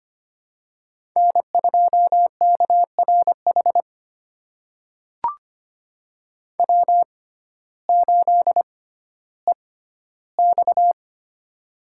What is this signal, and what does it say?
Morse code, heard as a keyed tone, about 25 wpm, spaced as for 14 wpm, 700 Hz. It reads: N2KRH W8EX
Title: USA Call Signs - 6x Character Spacing - 25wpm